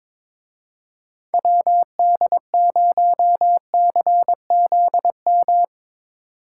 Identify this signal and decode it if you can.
Morse code, heard as a keyed tone, about 25 wpm, 700 Hz. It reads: WD0CZM
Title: Speed Racing - USA 2x3 Call Signs 22wpm